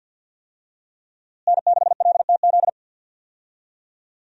Morse code, heard as a keyed tone, about 46 wpm, 700 Hz. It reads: N6LTZ
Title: USA Call Signs - 5 Characters - Repeated 3x - 50wpm